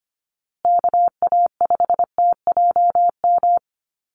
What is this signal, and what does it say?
Morse code, heard as a keyed tone, about 25 wpm, 700 Hz. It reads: KA5TJM
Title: USA Call Signs Repeated 3x 25wpm